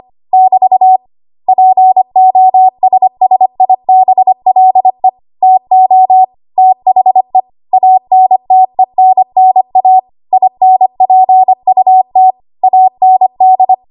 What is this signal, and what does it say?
Morse code, heard as a keyed tone, about 25 wpm, 750 Hz. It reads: <BT> POSSIBLE TO THE ANTENNA INPUT AND
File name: Archive / 25 WPM / 25WPM